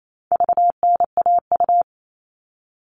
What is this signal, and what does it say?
Morse code, heard as a keyed tone, about 28 wpm, 700 Hz. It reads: VNAU
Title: All Cut Numbers - 4-Digit 28wpm